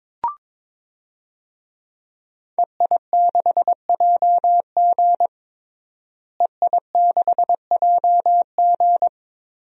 Morse code, heard as a keyed tone, about 22 wpm, 700 Hz. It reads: EI6JG EI6JG
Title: DXCC-100 Call Signs - 5 Characters - Repeated 3x - 22wpm